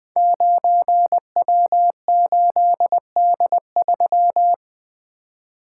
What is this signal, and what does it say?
Morse code, heard as a keyed tone, about 20 wpm, 700 Hz. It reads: 9W8D3
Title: Code Groups - 5 Letter:Numbers - 20wpm